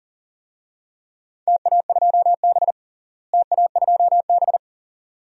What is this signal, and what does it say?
Morse code, heard as a keyed tone, about 40 wpm, 700 Hz. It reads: TA2B TA2B
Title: DXCC-100 Call Signs - 4 Characters - Repeated 2x - 40wpm